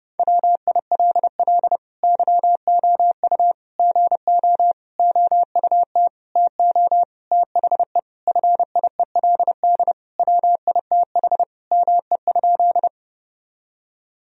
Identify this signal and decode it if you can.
Morse code, heard as a keyed tone, about 30 wpm, 700 Hz. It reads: WILL YOU GO OUT TO THE FIELD WITH ME?
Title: Sentences from Top 1000 Words - Review Part 2 - 30wpm